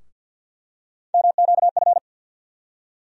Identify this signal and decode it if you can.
Morse code, heard as a keyed tone, about 53 wpm, 700 Hz. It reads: MQF
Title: USA Call Sign Postfixes - 3 Letters 50wpm